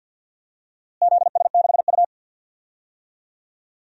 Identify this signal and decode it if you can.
Morse code, heard as a keyed tone, about 52 wpm, 700 Hz. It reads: ZS6U
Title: DXCC-100 Call Signs - 4 Characters - Repeated 2x - 50wpm